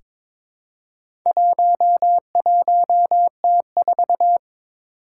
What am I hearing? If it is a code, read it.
Morse code, heard as a keyed tone, about 22 wpm, 700 Hz. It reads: 11T4
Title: Four Challenging Letter-Number Combinations 22wpm